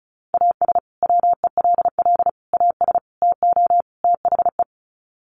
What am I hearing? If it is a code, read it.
Morse code, heard as a keyed tone, about 34 wpm, 700 Hz. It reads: AS WELL AS TO THE
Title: Sets of 5 Words Encore2 35wpm